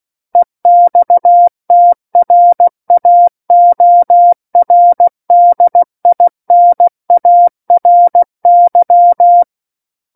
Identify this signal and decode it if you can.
Morse code, heard as a keyed tone, about 16 wpm, 700 Hz. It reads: EXTRAORDINARY
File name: Book One / 07-Chapter 7 - 16 WPM